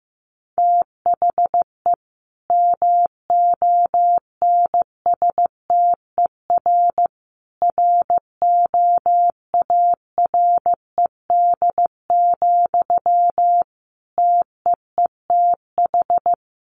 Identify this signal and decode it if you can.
Morse code, heard as a keyed tone, about 15 wpm, 700 Hz. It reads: THE MONSTER ROARED, TEETH
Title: A Sound Of Thunder 15wpm